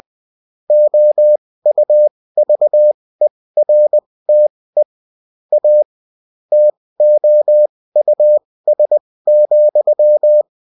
Morse code, heard as a keyed tone, about 20 wpm, spaced as for 16 wpm, 600 Hz. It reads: OUVERTE A TOUS,